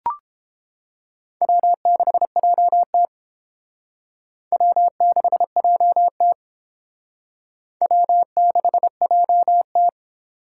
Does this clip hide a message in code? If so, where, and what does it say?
Yes: Morse code from 1.4 to 9.9 s, heard as a keyed tone, about 28 wpm, 700 Hz. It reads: W6JT W6JT W6JT